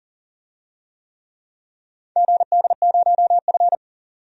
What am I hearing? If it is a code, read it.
Morse code, heard as a keyed tone, about 40 wpm, 700 Hz. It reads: GD0F
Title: DXCC-100 Call Signs - 4 Characters - 40wpm